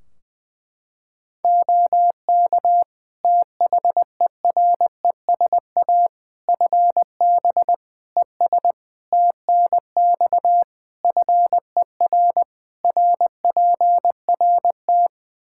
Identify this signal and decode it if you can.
Morse code, heard as a keyed tone, about 20 wpm, 700 Hz. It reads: OK THERESA FB ES TNX FER RPRT
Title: Practice QSO - Exchange 2 - Info - 20wpm